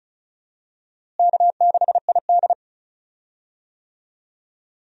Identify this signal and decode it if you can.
Morse code, heard as a keyed tone, about 34 wpm, 700 Hz. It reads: K6ID
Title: USA Call Signs - 4 Characters - Repeated 2x - 35wpm